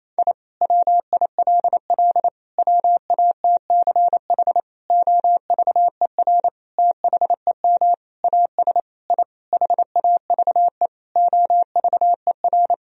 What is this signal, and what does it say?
Morse code, heard as a keyed tone, about 28 wpm, 700 Hz. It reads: I WILL WATCH OVER THEM AS I HAVE OVER
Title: Sentences from Top 1000 Words - Review Part 1 - 28wpm